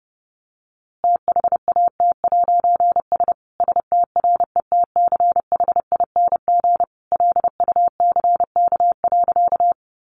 Morse code, heard as a keyed tone, about 30 wpm, 700 Hz. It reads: THAT'S STRETCHING LUCK.
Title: A Sound Of Thunder 30wpm